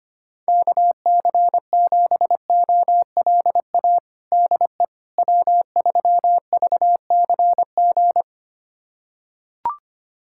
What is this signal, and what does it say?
Morse code, heard as a keyed tone, about 25 wpm, 700 Hz. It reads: KC7OLA DE W3VCG
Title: USA Callsign DE Callsign 25wpm